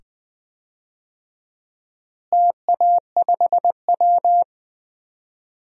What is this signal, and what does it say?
Morse code, heard as a keyed tone, about 20 wpm, 700 Hz. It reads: TA5W
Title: DXCC-100 Call Signs - 4 Characters - Repeated 2x - 20wpm